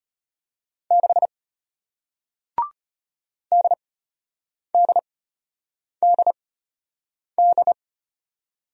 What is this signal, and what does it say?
Morse code, heard as a keyed tone, about 33 wpm, spaced as for 8 wpm, 700 Hz. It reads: 6 DDDD